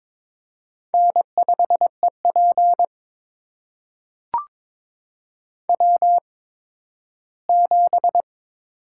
Morse code, heard as a keyed tone, about 22 wpm, spaced as for 13 wpm, 700 Hz. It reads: N5EP W7
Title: USA Call Signs - 4 Characters - 8x Character Spacing - 22wpm